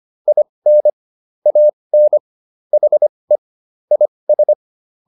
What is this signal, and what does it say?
Morse code, heard as a keyed tone, about 25 wpm, spaced as for 20 wpm, 600 Hz. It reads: IN AN HE IS